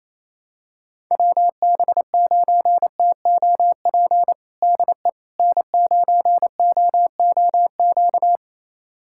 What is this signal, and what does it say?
Morse code, heard as a keyed tone, about 28 wpm, 700 Hz. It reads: WB9TOP DE N9OOQ
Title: USA Callsign DE Callsign 28wpm